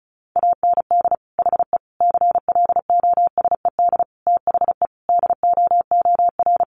Morse code, heard as a keyed tone, about 34 wpm, 700 Hz. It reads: AND HE CLOSED THE DOOR
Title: Sentences from Top 1500 Words 35wpm